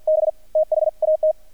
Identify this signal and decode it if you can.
Morse code, heard as a keyed tone, about 48 wpm, 620 Hz. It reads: B THAT
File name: archived / CRSnet